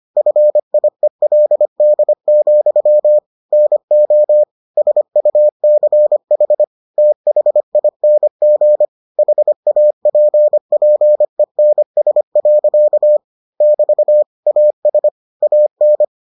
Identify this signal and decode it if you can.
Morse code, heard as a keyed tone, about 25 wpm, 600 Hz. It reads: FIELD, NO SUCH THING HAPPENS. <BT> AS AN